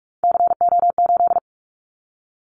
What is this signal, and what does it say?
Morse code, heard as a keyed tone, about 44 wpm, 700 Hz. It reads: CO8
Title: Callsign Prefixes for Top 100 DXCC - 45wpm